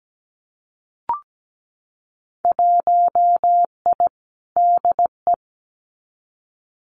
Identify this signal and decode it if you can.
Morse code, heard as a keyed tone, about 17 wpm, 700 Hz. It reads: 1I DE
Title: Winter Field Day Exchanges Encore 17wpm